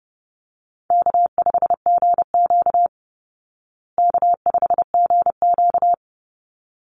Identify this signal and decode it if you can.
Morse code, heard as a keyed tone, about 30 wpm, 700 Hz. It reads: K5GQ K5GQ
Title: Call Signs Repeated 3x - 30wpm